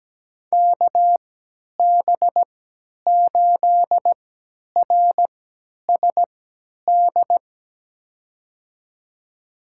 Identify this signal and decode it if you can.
Morse code, heard as a keyed tone, about 17 wpm, spaced as for 10 wpm, 700 Hz. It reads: KB8RSD
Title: USA Call Signs - 3x Character Spacing - 17wpm